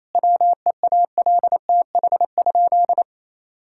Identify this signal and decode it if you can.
Morse code, heard as a keyed tone, about 28 wpm, 700 Hz. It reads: WEALTH?
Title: Sentences from Top 1600 Words 28wpm